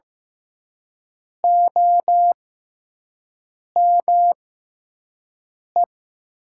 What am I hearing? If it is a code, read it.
Morse code, heard as a keyed tone, about 15 wpm, spaced as for 5 wpm, 700 Hz. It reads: OME
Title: 6x Character Spacing - Top 200 Words - 15wpm